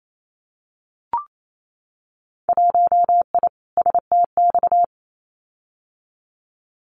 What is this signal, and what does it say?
Morse code, heard as a keyed tone, about 28 wpm, 700 Hz. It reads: 1I STX
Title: Winter Field Day Exchanges 28wpm